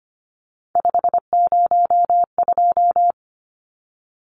Morse code, heard as a keyed tone, about 25 wpm, 700 Hz. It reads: 502
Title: Numbers - 3-Digit 25wpm